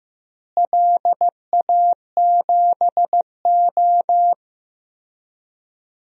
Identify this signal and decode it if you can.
Morse code, heard as a keyed tone, about 16 wpm, 700 Hz. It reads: LA7O
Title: DXCC-100 Call Signs - 4 Characters - Repeated 3x - 15wpm